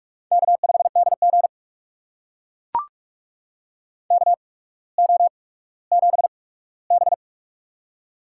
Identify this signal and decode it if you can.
Morse code, heard as a keyed tone, about 44 wpm, spaced as for 19 wpm, 690 Hz. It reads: K5DG KO7B
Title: USA Call Signs - 4 Characters - 8x Character Spacing - 45wpm